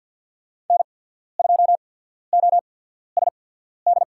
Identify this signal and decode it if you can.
Morse code, heard as a keyed tone, about 53 wpm, spaced as for 14 wpm, 700 Hz. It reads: N2OSD
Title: USA Call Signs - 5 Characters - 8x Character Spacing - 50wpm